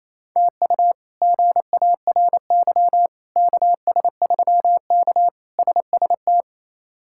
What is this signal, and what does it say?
Morse code, heard as a keyed tone, about 28 wpm, 700 Hz. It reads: TU GARY KS3K SST
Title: K1USN SST QSO Practice - 28wpm